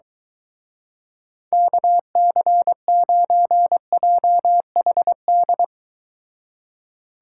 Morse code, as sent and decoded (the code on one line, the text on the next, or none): -.- -.-. ----. .--- .... -..
KC9JHD